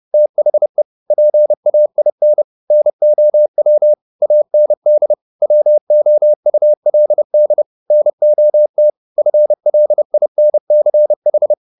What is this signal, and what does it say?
Morse code, heard as a keyed tone, about 30 wpm, 600 Hz. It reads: THE PAIN NOW AND WOULD NOT FLINCH